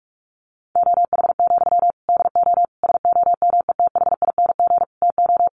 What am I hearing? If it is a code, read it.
Morse code, heard as a keyed tone, about 44 wpm, 700 Hz. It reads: OH, DO SOMETHING TO
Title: Sentences from Top 1100 Words 45wpm